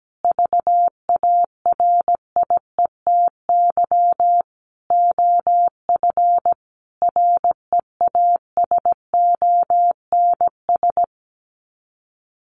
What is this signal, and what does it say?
Morse code, heard as a keyed tone, about 17 wpm, 700 Hz. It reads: VARIETY OF REASONS
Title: Sets of 5 Words Encore 17wpm